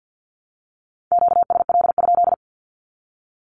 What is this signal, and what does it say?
Morse code, heard as a keyed tone, about 50 wpm, 700 Hz. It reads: QSL?
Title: Q-Codes 50wpm